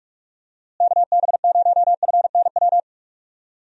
Morse code, heard as a keyed tone, about 45 wpm, 700 Hz. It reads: KB0FNW
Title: USA Call Signs - 6 Characters - Repeated 2x - 45wpm